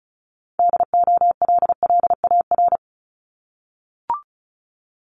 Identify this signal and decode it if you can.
Morse code, heard as a keyed tone, about 36 wpm, 700 Hz. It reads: DOLLAR
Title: Multi-letter Phonemes 35wpm